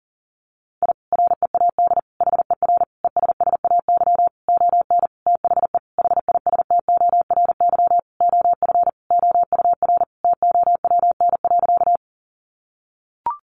Morse code, heard as a keyed tone, about 40 wpm, 700 Hz. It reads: I READ HER ESSAY ON THE HISTORY OF OUR TOWN.
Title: Sentences from Top 1500 Words 40wpm